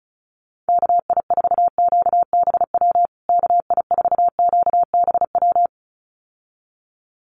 Morse code, heard as a keyed tone, about 34 wpm, 700 Hz. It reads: KI4QBW KI4QBW
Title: USA Call Signs Repeated 2x 35wpm